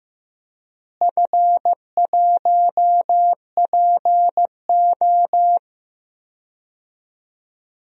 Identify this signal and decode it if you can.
Morse code, heard as a keyed tone, about 15 wpm, 700 Hz. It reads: F1PO
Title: DXCC-100 Call Signs - 4 Characters - Repeated 2x - 15wpm